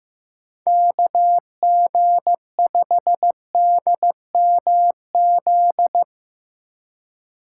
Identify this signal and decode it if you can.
Morse code, heard as a keyed tone, about 15 wpm, 700 Hz. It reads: KG5DMZ
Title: USA Call Signs - 6 Characters - 4x Character Spacing - 15wpm